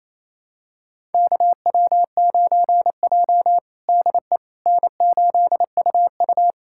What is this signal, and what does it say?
Morse code, heard as a keyed tone, about 27 wpm, 700 Hz. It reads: KW9J DE N8UU
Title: USA Callsign DE Callsign - 4 Characters - 28wpm